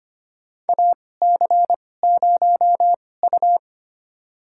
Morse code, heard as a keyed tone, about 25 wpm, spaced as for 18 wpm, 700 Hz. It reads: AC0U
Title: USA Call Signs - 4 Characters - 2x Character Spacing - 25wpm